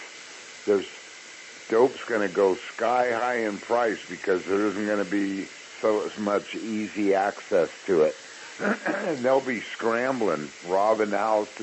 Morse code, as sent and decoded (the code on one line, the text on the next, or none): none